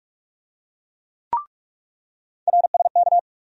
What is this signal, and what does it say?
Morse code, heard as a keyed tone, about 42 wpm, 700 Hz. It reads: RSK
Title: USA Call Sign Postfixes - 3 Letters 45wpm